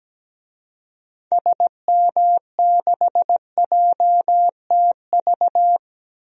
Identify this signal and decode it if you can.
Morse code, heard as a keyed tone, about 17 wpm, 700 Hz. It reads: SM6JTV
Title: DXCC-100 Call Signs - 6 Characters - Repeated 2x - 17wpm